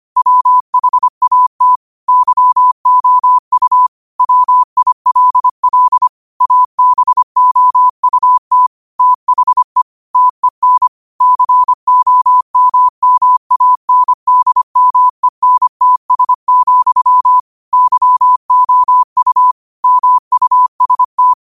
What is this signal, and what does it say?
Morse code, heard as a keyed tone, about 25 wpm, 1000 Hz. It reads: WHAT YOU WILL ABOUT THE TEN COMMANDMENTS, YOU MUST
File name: qotdfiles / 25wpm